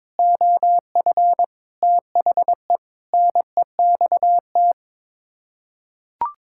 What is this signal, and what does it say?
Morse code, heard as a keyed tone, about 22 wpm, 700 Hz. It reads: OF THE NEXT
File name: practice / 22wpm / Sets of 6 Words - 22wpm